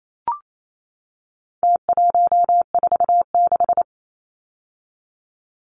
Morse code, heard as a keyed tone, about 28 wpm, 700 Hz. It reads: T146
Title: Numbers - Contest Style - 4-Digit 28wpm